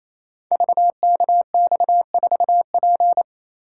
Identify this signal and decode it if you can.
Morse code, heard as a keyed tone, about 29 wpm, 700 Hz. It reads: VKX4P
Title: Code Groups - 5 Letter:Numbers - 28wpm